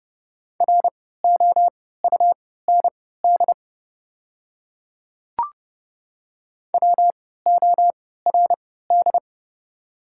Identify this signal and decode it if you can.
Morse code, heard as a keyed tone, about 30 wpm, spaced as for 17 wpm, 700 Hz. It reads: ROUND WORD
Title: 3x Character Spacing - Top 200 Words - 30wpm